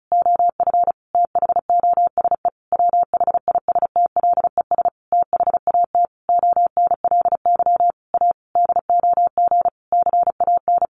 Morse code, heard as a keyed tone, about 34 wpm, 700 Hz. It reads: OF THOSE WHISTLES THAT ONLY A DOG CAN
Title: A Sound Of Thunder 35wpm